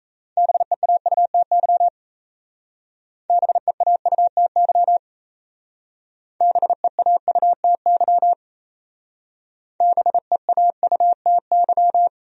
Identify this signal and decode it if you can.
Morse code, heard as a keyed tone, about 35 wpm, 700 Hz. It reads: BEAUTY BEAUTY BEAUTY BEAUTY